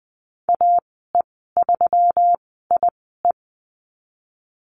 Morse code, heard as a keyed tone, about 20 wpm, spaced as for 15 wpm, 700 Hz. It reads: AE3IE
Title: 147 Lesson 24g - Letter y - Call Signs 20wpm